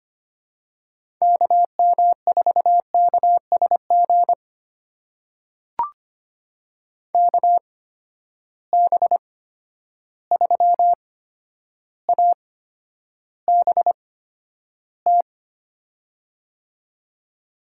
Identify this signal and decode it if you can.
Morse code, heard as a keyed tone, about 25 wpm, spaced as for 11 wpm, 700 Hz. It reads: KM4KSG KB3ABT
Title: USA Call Signs - 6 Characters - 8x Character Spacing - 25wpm